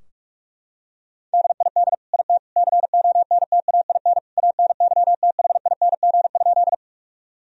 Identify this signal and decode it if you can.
Morse code, heard as a keyed tone, about 47 wpm, 700 Hz. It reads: DID IT CONTAIN ANYTHING?
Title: Sentences from Top 1000 Words - Review Part 2 - 45wpm